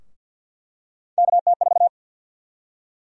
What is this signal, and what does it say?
Morse code, heard as a keyed tone, about 48 wpm, 700 Hz. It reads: KT4
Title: USA Call Sign Prefixes - 50wpm